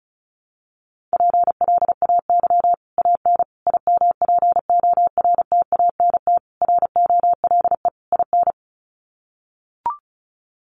Morse code, heard as a keyed tone, about 34 wpm, 700 Hz. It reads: PLAY AN IMPORTANT ROLE IN
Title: Sets of 5 Words Encore 35wpm